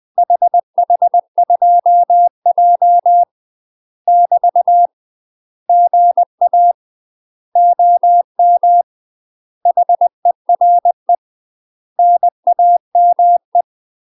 Morse code, heard as a keyed tone, about 20 wpm, 700 Hz. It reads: HH2J <BT> GA OM HERE NAME
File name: UK MASTER QSOs TRAINING / 20WPM / UK MASTER QSOs 0035